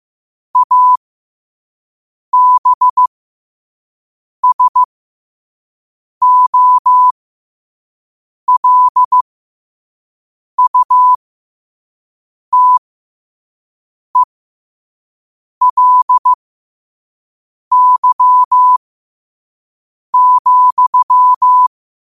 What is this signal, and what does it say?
Morse code, heard as a keyed tone, about 15 wpm, spaced as for 5 wpm, 1000 Hz. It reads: ABSOLUTELY,
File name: qotdfiles / 5wpm